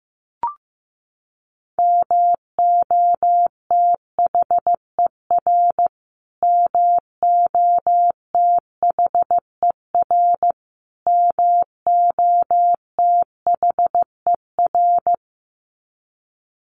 Morse code, heard as a keyed tone, about 15 wpm, 700 Hz. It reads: MOTHER MOTHER MOTHER
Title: Top 500 Words Repeated 15wpm